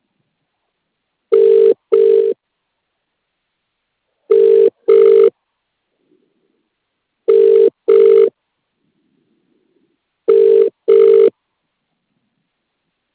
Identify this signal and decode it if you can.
Ringback tone (UK), 400 + 450 Hz.